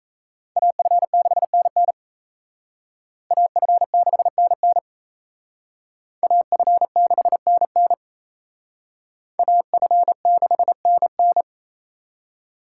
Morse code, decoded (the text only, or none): AF6NN AF6NN AF6NN AF6NN